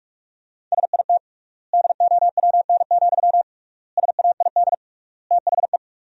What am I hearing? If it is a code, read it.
Morse code, heard as a keyed tone, about 46 wpm, 700 Hz. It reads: SIT DOWN, SAID THE
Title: 3x Word-Spacing - Sentences from Top 500 Words - 45wpm